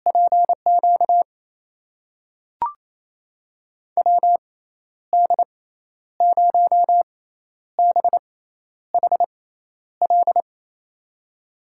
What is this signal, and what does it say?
Morse code, heard as a keyed tone, about 28 wpm, spaced as for 11 wpm, 700 Hz. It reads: PQ WD0BHL
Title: USA Call Signs - 6x Character Spacing - 28wpm